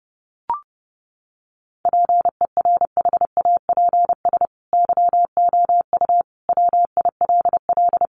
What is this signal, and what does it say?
Morse code, heard as a keyed tone, about 30 wpm, 700 Hz. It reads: PERHAPS YOU WILL
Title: Sentences from Top 1400 Words 30wpm